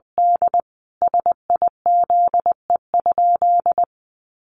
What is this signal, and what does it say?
Morse code, heard as a keyed tone, about 20 wpm, 700 Hz. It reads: D SIZE?
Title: Sentences from Top 500 Words 20wpm